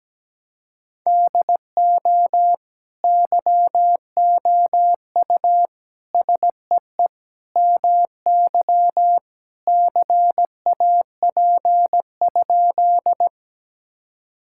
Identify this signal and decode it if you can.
Morse code, heard as a keyed tone, about 17 wpm, 700 Hz. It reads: DO YOU SEE MY CAP?
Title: Sentences from Top 1600 Words 17wpm